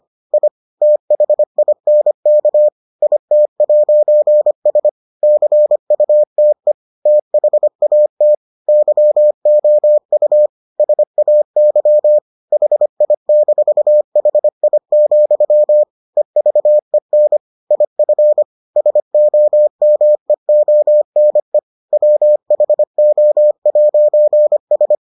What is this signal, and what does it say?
Morse code, heard as a keyed tone, about 25 wpm, 600 Hz. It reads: I THINK IT'S CUTE THAT YOU SAY HI-HI, EVEN IF SOMEONE WHO'S